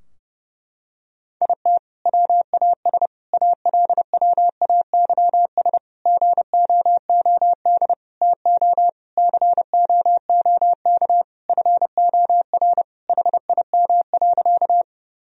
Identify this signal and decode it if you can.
Morse code, heard as a keyed tone, about 32 wpm, 700 Hz. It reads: IT WAS ALWAYS GOOD TO COOK FOR HIM.
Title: Sentences from Top 1000 Words - Review Part 4 - 30wpm